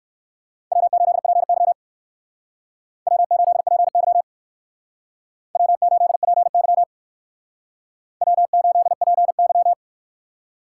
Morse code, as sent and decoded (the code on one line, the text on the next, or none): .-- ---.. .--. -.-- / .-- ---.. .--. -.-- / .-- ---.. .--. -.-- / .-- ---.. .--. -.--
W8PY W8PY W8PY W8PY